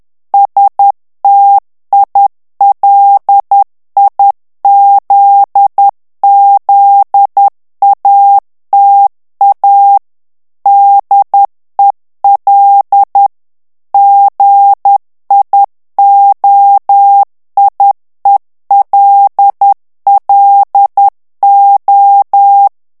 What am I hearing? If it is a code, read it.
Morse code, heard as a keyed tone, about 10 wpm, 800 Hz. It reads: STILIZZATA DEL GIOIELLO
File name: morse mp3 / ITALIEN texte clair 1 vitesse 10 mots minute